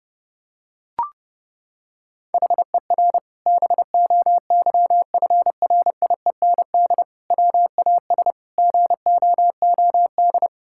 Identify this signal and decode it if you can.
Morse code, heard as a keyed tone, about 28 wpm, 700 Hz. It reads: HER BOYFRIEND WAS GOOD
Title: Sentences from Top 1000 Words - Review Part 5 - 30wpm